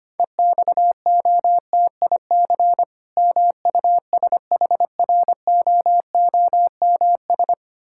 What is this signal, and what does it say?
Morse code, heard as a keyed tone, about 25 wpm, 690 Hz. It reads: EXOTIC MUSHROOMS
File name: practice / 25wpm / Difficult Words with Hints - 25wpm